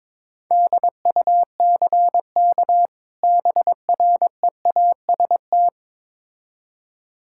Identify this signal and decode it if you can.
Morse code, heard as a keyed tone, about 22 wpm, 700 Hz. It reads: DUCK BREAST
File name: Difficult Words with Hints - 22wpm